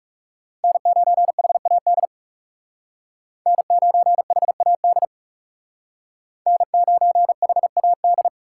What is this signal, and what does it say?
Morse code, heard as a keyed tone, about 36 wpm, 700 Hz. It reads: N9HAD N9HAD N9HAD
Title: Speed Racing - USA 1x3 Call Signs 30wpm